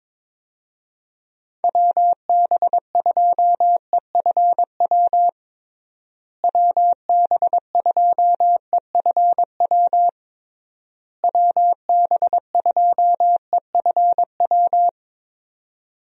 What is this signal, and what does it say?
Morse code, heard as a keyed tone, about 22 wpm, 700 Hz. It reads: WB2EFW WB2EFW WB2EFW
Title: USA Call Signs - 6 Characters - Repeated 3x - 22wpm